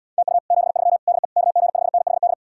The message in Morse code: TU 73 DE KG5NHD

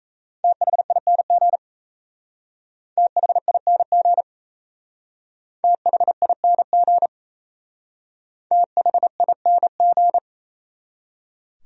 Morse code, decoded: THING THING THING THING

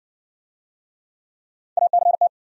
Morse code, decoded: ACT

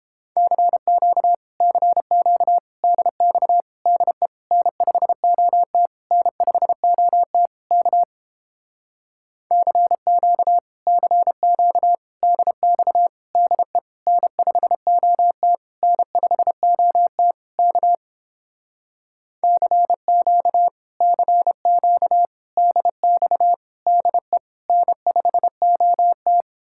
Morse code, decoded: CQ CQ DX DE N5OT N5OT K CQ CQ DX DE N5OT N5OT K CQ CQ DX DE N5OT

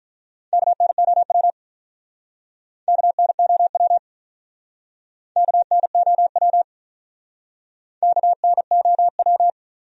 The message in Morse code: KNOW KNOW KNOW KNOW

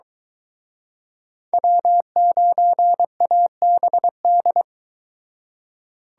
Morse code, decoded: W9ABD

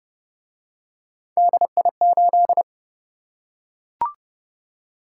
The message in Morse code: DI8

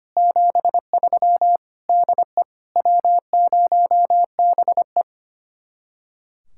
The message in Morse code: 73 DE W0BE

